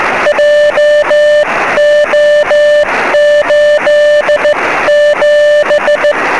Morse code, heard as a keyed tone, about 16 wpm, spaced as for 14 wpm, 590 Hz, with noise. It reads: JO87
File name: sm7lcb / 06-sk7ca 23cm